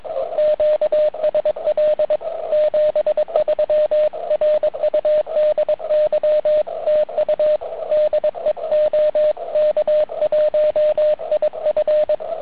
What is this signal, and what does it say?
Morse code, heard as a keyed tone, about 21 wpm, 620 Hz, with radio noise. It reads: QSL 73 RUDY TU DE OK1IF